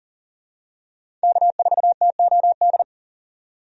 Morse code, heard as a keyed tone, about 40 wpm, 700 Hz. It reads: K4TOD
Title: USA Call Signs - 5 Characters - 40wpm